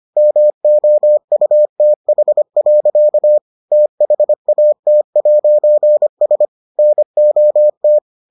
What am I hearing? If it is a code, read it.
Morse code, heard as a keyed tone, about 25 wpm, 600 Hz. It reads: MOUTH. THAT'S NOT